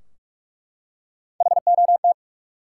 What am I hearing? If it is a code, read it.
Morse code, heard as a keyed tone, about 44 wpm, 700 Hz. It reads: HOT
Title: Three Letter Words - 45wpm